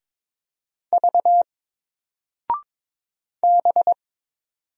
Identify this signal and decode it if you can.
Morse code, heard as a keyed tone, about 22 wpm, 700 Hz. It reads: V B
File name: Single-Letter-Number-22wpm